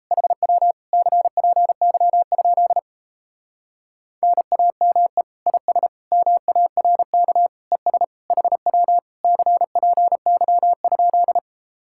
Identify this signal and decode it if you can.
Morse code, heard as a keyed tone, about 37 wpm, 700 Hz. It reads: HW CPY? NAME IS MARK ES HW CPY?